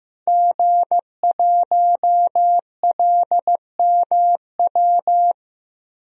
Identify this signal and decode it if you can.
Morse code, heard as a keyed tone, about 15 wpm, 700 Hz. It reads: G1LMW
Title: DXCC-100 Call Signs - 5 Characters - Repeated 2x - 15wpm